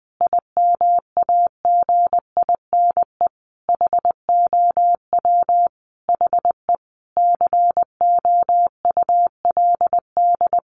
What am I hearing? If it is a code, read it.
Morse code, heard as a keyed tone, about 20 wpm, 700 Hz. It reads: IMAGINE HOW HE COULD
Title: Sentences from Top 1400 Words 20wpm